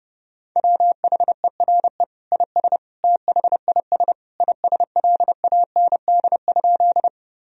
Morse code, heard as a keyed tone, about 28 wpm, 700 Hz. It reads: WHERE IS THIS ISLAND?
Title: Sentences from Top 1000 Words - Review Part 3 - 30wpm